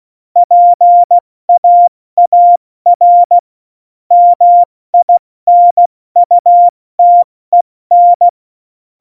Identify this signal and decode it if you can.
Morse code, heard as a keyed tone, about 16 wpm, 700 Hz. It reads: PAAR MINUTEN